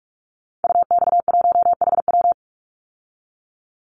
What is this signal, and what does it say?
Morse code, heard as a keyed tone, about 44 wpm, 700 Hz. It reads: UX1HW
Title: Call Signs Repeated 3x - 45wpm